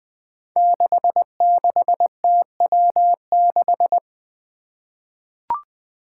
Morse code, heard as a keyed tone, about 20 wpm, 700 Hz. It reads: 66TW6
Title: Five Challenging Letter-Number Combinations 20wpm